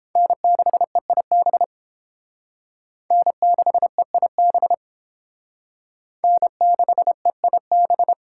Morse code, heard as a keyed tone, about 30 wpm, 700 Hz. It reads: N6EIB N6EIB N6EIB